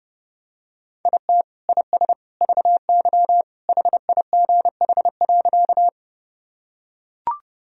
Morse code, heard as a keyed tone, about 30 wpm, 700 Hz. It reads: IT IS VY HIGH.
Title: Sentences from Top 200 Words and Abbreviations 30wpm